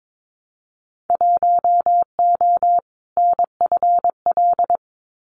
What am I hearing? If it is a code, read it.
Morse code, heard as a keyed tone, about 22 wpm, 700 Hz. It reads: 1O NFL
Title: Winter Field Day Exchanges 22wpm